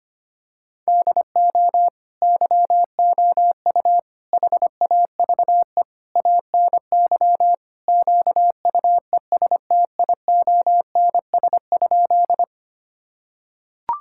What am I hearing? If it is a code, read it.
Morse code, heard as a keyed tone, about 25 wpm, 700 Hz. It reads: DO YOU HAVE ANY QUESTIONS?
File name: practice / 25wpm / Sentences with 4-Word Phrases - 25wpm